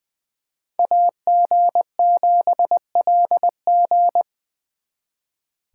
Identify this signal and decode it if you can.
Morse code, heard as a keyed tone, about 25 wpm, 700 Hz. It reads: AG7LG
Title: Speed Racing - USA 2x2 Call Signs 20wpm